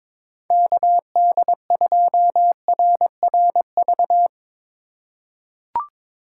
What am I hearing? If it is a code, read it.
Morse code, heard as a keyed tone, about 22 wpm, 700 Hz. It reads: KD2RRV
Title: USA Call Signs - 6 Characters - 4x Character Spacing - 22wpm